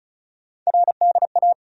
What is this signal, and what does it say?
Morse code, heard as a keyed tone, about 34 wpm, 700 Hz. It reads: RDA
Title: USA Call Sign Postfixes - 3 Letters 35wpm